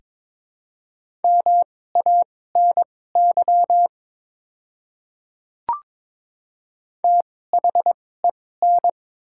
Morse code, heard as a keyed tone, about 22 wpm, spaced as for 16 wpm, 700 Hz. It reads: MANY THEN
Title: 2x Character Spacing - Top 100 Words - 22wpm